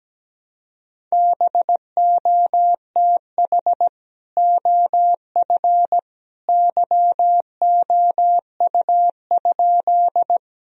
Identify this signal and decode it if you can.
Morse code, heard as a keyed tone, about 17 wpm, 700 Hz. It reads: BOTH OF YOU?
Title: Sentences from Top 1000 Words - Review Part 1 - 17wpm